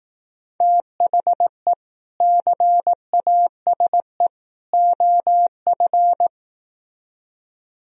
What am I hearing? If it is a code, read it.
Morse code, heard as a keyed tone, about 18 wpm, 700 Hz. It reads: THE CASE OF
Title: Speed-Racing - Sets of 3 Words 15wpm